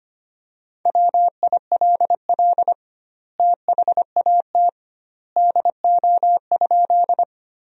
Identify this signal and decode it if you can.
Morse code, heard as a keyed tone, about 25 wpm, 700 Hz. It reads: WILL THAT DO?
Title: 2x Word-Spacing - Sentences from Top 100 Words - 25wpm